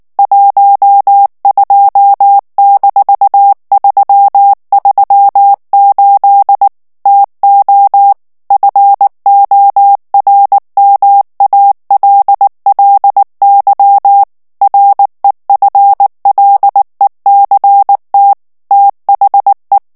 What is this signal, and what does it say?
Morse code, heard as a keyed tone, about 19 wpm, 800 Hz. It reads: 12-338 TO FORMALLY REFLECT THE
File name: ANGLAIS texte clair 3 vitesse 18 mots minute